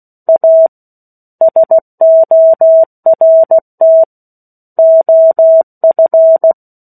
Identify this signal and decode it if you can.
Morse code, heard as a keyed tone, about 16 wpm, 650 Hz. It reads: A SORT OF
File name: 03-Chapter-03 16 WPM